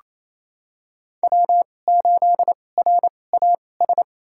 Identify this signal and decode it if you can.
Morse code, heard as a keyed tone, about 28 wpm, spaced as for 20 wpm, 700 Hz. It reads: W8RAS
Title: USA Call Signs - 5 Characters - 2x Character Spacing - 28wpm